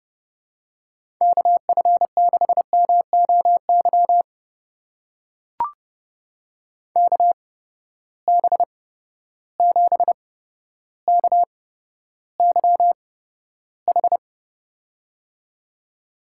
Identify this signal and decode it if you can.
Morse code, heard as a keyed tone, about 31 wpm, spaced as for 13 wpm, 700 Hz. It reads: KF6MOY KB7KYH
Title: USA Call Signs - 6 Characters - 8x Character Spacing - 30wpm